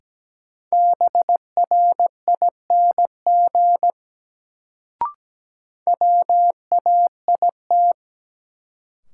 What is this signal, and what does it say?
Morse code, heard as a keyed tone, about 17 wpm, 700 Hz. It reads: BRING WAIT
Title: Sets of 5 Words Encore - New Word Drill - 17wpm